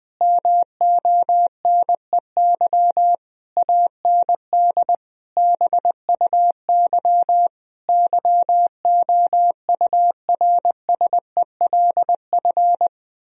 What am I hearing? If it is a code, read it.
Morse code, heard as a keyed tone, about 20 wpm, 700 Hz. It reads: MONEY AND BUY YOURSELF
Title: Sentences from Top 1000 Words - Review Part 4 - 20wpm